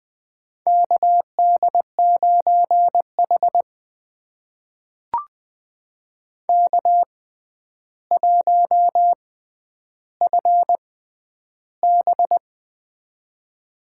Morse code, heard as a keyed tone, about 20 wpm, spaced as for 10 wpm, 700 Hz. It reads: KD9H K1FB
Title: USA Call Signs - 4 Characters - 6x Character Spacing - 20wpm